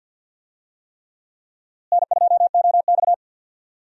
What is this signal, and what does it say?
Morse code, heard as a keyed tone, about 52 wpm, 700 Hz. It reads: N2OX
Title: USA Call Signs - 4 Characters - Repeated 3x - 50wpm